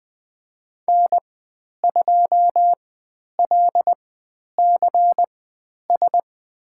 Morse code, heard as a keyed tone, about 20 wpm, spaced as for 10 wpm, 700 Hz. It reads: N2LCS
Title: 095 Lesson 16g - Letter c - Call Signs 20wpm